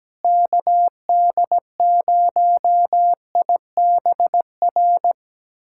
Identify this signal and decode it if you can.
Morse code, heard as a keyed tone, about 17 wpm, 700 Hz. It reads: KD0IBR
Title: USA Callsign DE Callsign 17wpm